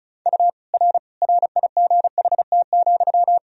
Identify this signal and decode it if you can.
Morse code, heard as a keyed tone, about 36 wpm, 700 Hz. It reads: U R RIGHT,
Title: Sentences from Top 400 Words and Abbreviations 35wpm